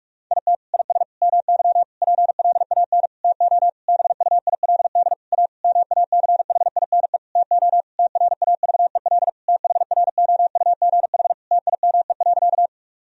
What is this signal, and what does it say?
Morse code, heard as a keyed tone, about 41 wpm, 700 Hz. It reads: IT IS MY PLAN TO BUILD A MACHINE TO TRAVEL THROUGH TIME.
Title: Sentences from Top 1000 Words - Review Part 2 - 45wpm